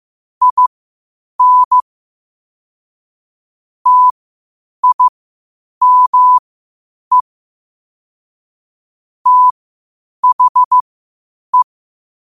Morse code, heard as a keyed tone, about 15 wpm, spaced as for 8 wpm, 1000 Hz. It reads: IN TIME THE